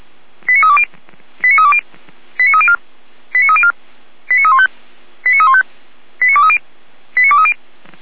Five-tone selective call (ZVEI), heard as quick runs of five tones, tones of about 70 ms. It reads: 89320 89320 89304 89304 89315 89315 89230 89230